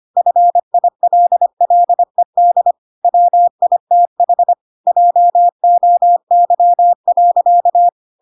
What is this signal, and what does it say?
Morse code, heard as a keyed tone, about 25 wpm, 700 Hz. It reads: FILLED WITH JOY.
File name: Pinocchio / Pinocchio ch1 25 25 ik1hge